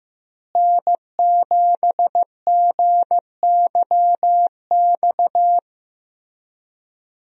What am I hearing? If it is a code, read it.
Morse code, heard as a keyed tone, about 15 wpm, 700 Hz. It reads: N7GYX
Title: USA Call Signs - 5 Characters - Repeated 3x - 15wpm